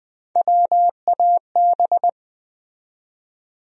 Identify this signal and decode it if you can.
Morse code, heard as a keyed tone, about 20 wpm, 700 Hz. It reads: WAB
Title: USA Call Sign Postfixes - 3 Letters 20wpm